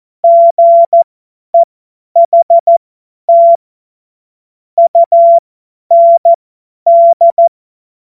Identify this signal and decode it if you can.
Morse code, heard as a keyed tone, about 14 wpm, spaced as for 10 wpm, 680 Hz. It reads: GEHT UND
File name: Morsekurs / Witze / Witz 16. Februar 2021, 50 F70